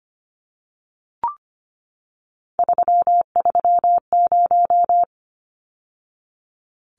Morse code, heard as a keyed tone, about 25 wpm, 700 Hz. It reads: 330
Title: Numbers - 3-Digit 25wpm